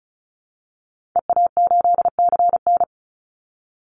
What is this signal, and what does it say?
Morse code, heard as a keyed tone, about 34 wpm, 700 Hz. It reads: EA8CN
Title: Call Signs Repeated 2x - 35wpm